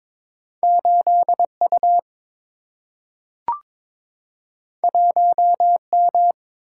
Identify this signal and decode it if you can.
Morse code, heard as a keyed tone, about 22 wpm, 700 Hz. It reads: 8 U 1 M
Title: Single-Letter Single-Number 22wpm